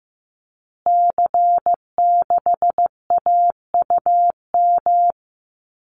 Morse code, heard as a keyed tone, about 15 wpm, 700 Hz. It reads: C6AUM
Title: Call Signs Repeated 3x - 15wpm